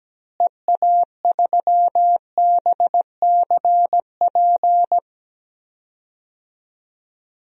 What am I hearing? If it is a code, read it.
Morse code, heard as a keyed tone, about 17 wpm, 700 Hz. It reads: EA3BCP
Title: DXCC-100 Call Signs - 6 Characters - Repeated 3x - 17wpm